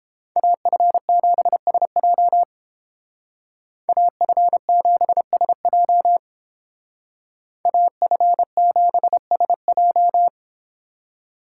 Morse code, heard as a keyed tone, about 30 wpm, 700 Hz. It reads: AF7SJ AF7SJ AF7SJ